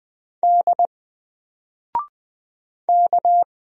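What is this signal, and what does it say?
Morse code, heard as a keyed tone, about 20 wpm, 700 Hz. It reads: D K
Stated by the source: Single letter and number Morse drill at 20 wpm, from N4LCD.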